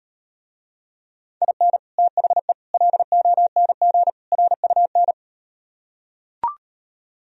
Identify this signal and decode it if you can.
Morse code, heard as a keyed tone, about 39 wpm, 700 Hz. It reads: IN THE LONG RUN